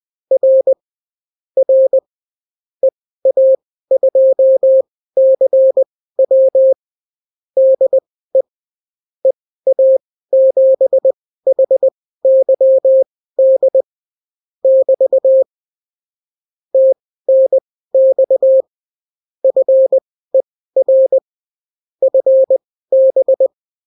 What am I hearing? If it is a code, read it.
Morse code, heard as a keyed tone, about 20 wpm, spaced as for 14 wpm, 530 Hz. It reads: R R EA2CW DE EA7HYD <BT> TNX FER FB
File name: CW QSO libroCW4 20-10ppm